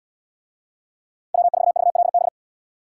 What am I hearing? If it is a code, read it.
Morse code, heard as a keyed tone, about 72 wpm, 700 Hz. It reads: W4VLB